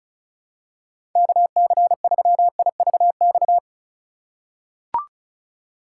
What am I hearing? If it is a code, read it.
Morse code, heard as a keyed tone, about 34 wpm, 700 Hz. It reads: KC3IVX
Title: USA Call Signs - 6 Characters - 4x Character Spacing - 35wpm